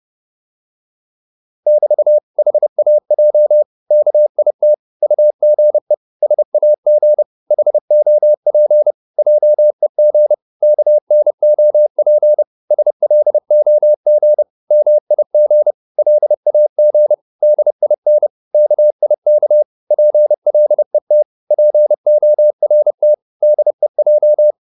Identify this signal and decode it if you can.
Morse code, heard as a keyed tone, about 30 wpm, 600 Hz. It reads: <BT> HAJ KIT UGE SAG HOP JEG KNOP SLOG MIG LAG DIN KIK PLET PORT DEJ
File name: Korte ord DK 30wpm